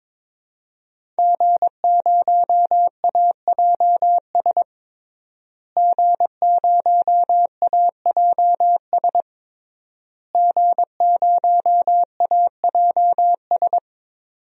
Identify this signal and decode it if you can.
Morse code, heard as a keyed tone, about 22 wpm, 700 Hz. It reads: G0AJS G0AJS G0AJS
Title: DXCC-100 Call Signs - 5 Characters - Repeated 3x - 22wpm